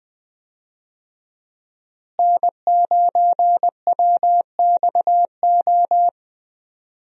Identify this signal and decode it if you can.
Morse code, heard as a keyed tone, about 20 wpm, 700 Hz. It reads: N9WXO
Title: USA Call Signs - 5 Characters - Repeated 3x - 20wpm